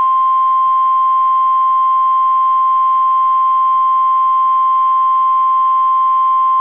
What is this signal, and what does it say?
Emergency Alert System (SAME) tones (the 1050 Hz attention tone). It is heard from 1.0 to 6.0 s.